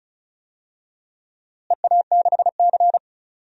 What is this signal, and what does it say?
Morse code, heard as a keyed tone, about 36 wpm, 700 Hz. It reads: EA6C
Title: DXCC-100 Call Signs - 4 Characters - Repeated 2x - 35wpm